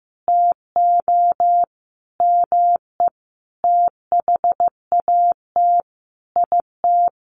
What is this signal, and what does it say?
Morse code, heard as a keyed tone, about 15 wpm, 700 Hz. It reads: TO ME THAT IT
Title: Sentences from Top 1500 Words 15wpm